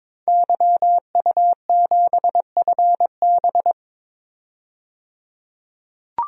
Morse code, heard as a keyed tone, about 22 wpm, 700 Hz. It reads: YU7FB
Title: DXCC-100 Call Signs - 5 Characters - 22wpm